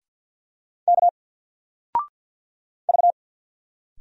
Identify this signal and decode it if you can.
Morse code, heard as a keyed tone, about 50 wpm, 700 Hz. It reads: K V
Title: Single Character (Letter:Number:Punctuation:Prosign) 50wpm